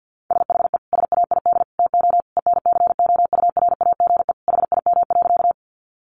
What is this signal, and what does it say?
Morse code, heard as a keyed tone, about 50 wpm, 700 Hz. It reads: SHE SAID TO ENCOURAGE HIM.
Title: Sentences from Top 1400 Words 50wpm